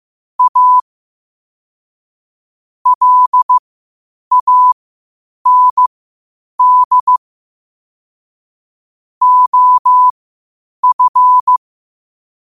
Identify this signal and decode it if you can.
Morse code, heard as a keyed tone, about 15 wpm, spaced as for 8 wpm, 1000 Hz. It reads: A LAND OF
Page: Quotes for Fri, 15 Aug 2025 in Morse Code at 8 words per minute.